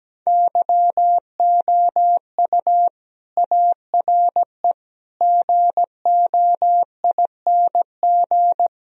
Morse code, heard as a keyed tone, about 17 wpm, 700 Hz. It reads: YOU ARE GOING
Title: Speed-Racing - Sets of 4 Words 17wpm